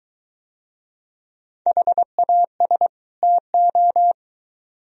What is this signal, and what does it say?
Morse code, heard as a keyed tone, about 22 wpm, 700 Hz. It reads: HAS TO